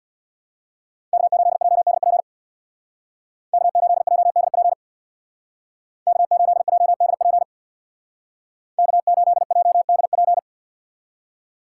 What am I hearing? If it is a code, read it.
Morse code, heard as a keyed tone, about 64 wpm, 700 Hz. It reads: K8JDP K8JDP K8JDP K8JDP